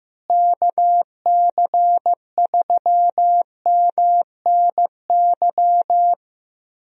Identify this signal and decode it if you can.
Morse code, heard as a keyed tone, about 15 wpm, 700 Hz. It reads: KC3MNY
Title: USA Call Signs - 6 Characters - Repeated 2x - 15wpm